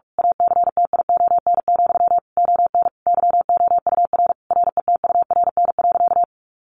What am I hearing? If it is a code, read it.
Morse code, heard as a keyed tone, about 44 wpm, 700 Hz. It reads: ACTION, ON YOUR RETURN.
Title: A Sound Of Thunder 45wpm